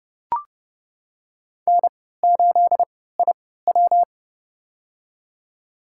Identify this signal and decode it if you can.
Morse code, heard as a keyed tone, about 30 wpm, spaced as for 17 wpm, 700 Hz. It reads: N8IW
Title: USA Call Signs - 4 Characters - 3x Character Spacing - 30wpm